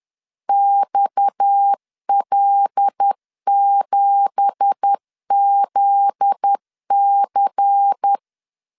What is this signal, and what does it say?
Morse code, heard as a keyed tone, about 10 wpm, 800 Hz. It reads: XL7ZC
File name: LEZIONI / 10 SIGNS / 10A SLASH 10wpm 2808